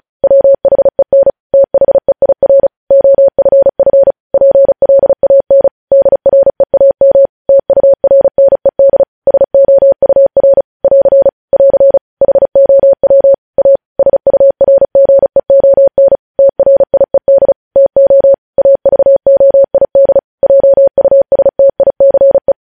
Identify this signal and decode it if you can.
Morse code, heard as a keyed tone, about 36 wpm, 550 Hz. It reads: WHEN THEIR OFF PLAN DREAM TURNED SOUR <AR> <AR> HOW A SURGEON TRIED TO AVOID JUSTICE
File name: podcast / News headlines-1x-35x35